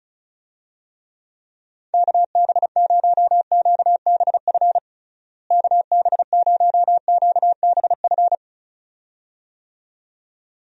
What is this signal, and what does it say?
Morse code, heard as a keyed tone, about 34 wpm, 700 Hz. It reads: KB0QBF KB0QBF
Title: USA Call Signs - 6 Characters - Repeated 2x - 35wpm